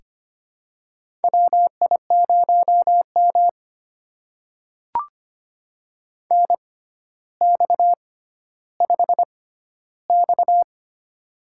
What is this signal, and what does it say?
Morse code, heard as a keyed tone, about 25 wpm, spaced as for 13 wpm, 700 Hz. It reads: WI0M NX5X
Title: USA Call Signs - 4 Characters - 6x Character Spacing - 25wpm